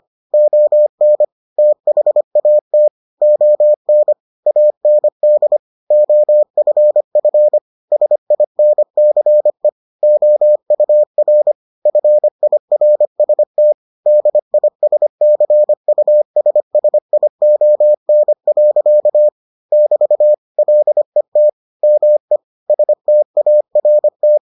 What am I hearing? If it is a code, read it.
Morse code, heard as a keyed tone, about 25 wpm, 600 Hz. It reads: ON THAT ON AND OFF SINCE OUR FIRST DISCUSSION. <BT> LET ME START